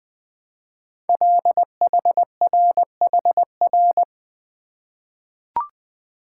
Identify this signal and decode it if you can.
Morse code, heard as a keyed tone, about 20 wpm, 700 Hz. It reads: LHRHR
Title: Code Groups - 5 Letter:Numbers - 20wpm